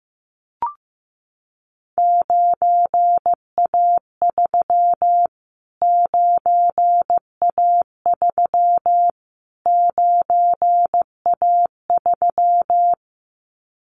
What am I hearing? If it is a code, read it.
Morse code, heard as a keyed tone, about 15 wpm, 700 Hz. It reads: 9A3 9A3 9A3
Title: Callsign Prefixes for Top 25 DXCC - Repeated 3x - 15wpm